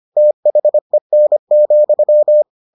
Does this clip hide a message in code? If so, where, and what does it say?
Yes: Morse code from 0.2 to 2.4 s, heard as a keyed tone, about 25 wpm, 600 Hz. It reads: THEN,